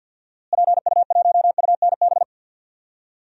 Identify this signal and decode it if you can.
Morse code, heard as a keyed tone, about 52 wpm, 700 Hz. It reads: PU1UNB
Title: DXCC-100 Call Signs - 6 Characters - Repeated 2x - 50wpm